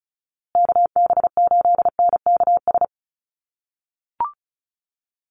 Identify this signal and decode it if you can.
Morse code, heard as a keyed tone, about 34 wpm, 700 Hz. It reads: KB8NKS
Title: USA Call Signs Repeated 3x 35wpm